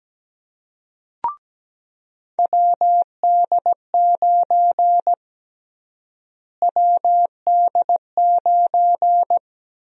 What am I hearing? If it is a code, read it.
Morse code, heard as a keyed tone, about 17 wpm, 700 Hz. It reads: WD9 WD9
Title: USA Call Sign Prefixes - Repeated 2x - 17wpm